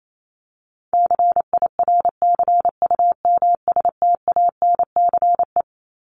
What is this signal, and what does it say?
Morse code, heard as a keyed tone, about 28 wpm, 700 Hz. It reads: CIRCUMSTANCE
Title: ES:S Suffix - Top 100 Words with Base Words 28wpm